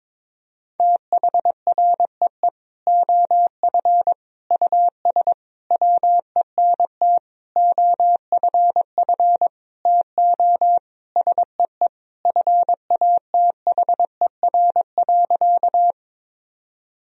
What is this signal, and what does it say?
Morse code, heard as a keyed tone, about 22 wpm, 700 Hz. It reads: THREE OF US WENT OFF TO SEE FATHER.
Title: Sentences from Top 1000 Words - Review Part 1 - 22wpm